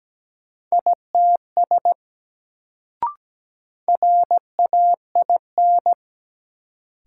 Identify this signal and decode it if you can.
Morse code, heard as a keyed tone, about 17 wpm, 700 Hz. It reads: ITS RAIN
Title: Sentences from Top 600 Words - New Word Drill - 17wpm